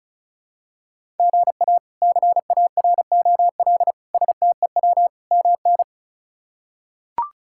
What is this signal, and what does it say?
Morse code, heard as a keyed tone, about 36 wpm, 700 Hz. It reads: GA CAROL STEW MN
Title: K1USN SST QSO Practice - 35wpm